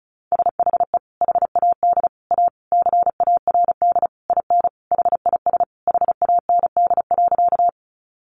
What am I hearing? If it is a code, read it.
Morse code, heard as a keyed tone, about 34 wpm, 700 Hz. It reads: SHE HAD A CARD IN HIS HAND.
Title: Sentences from Top 600 Words 35wpm